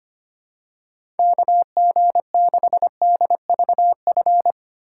Morse code, heard as a keyed tone, about 25 wpm, 700 Hz. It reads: KG6DVF
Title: USA Call Signs - 6 Characters - 4x Character Spacing - 25wpm